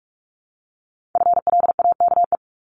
Morse code, heard as a keyed tone, about 46 wpm, 700 Hz. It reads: FLAKE
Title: Multi-letter Phonemes 45wpm